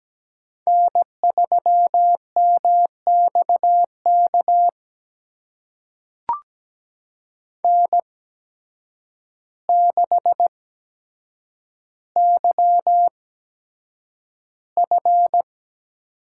Morse code, heard as a keyed tone, about 17 wpm, spaced as for 8 wpm, 700 Hz. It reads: N3MXK N6YF
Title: USA Call Signs - 5 Characters - 8x Character Spacing - 17wpm